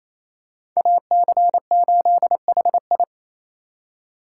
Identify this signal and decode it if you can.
Morse code, heard as a keyed tone, about 28 wpm, 700 Hz. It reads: AC8HI
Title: USA Call Signs - 5 Characters - 2x Character Spacing - 28wpm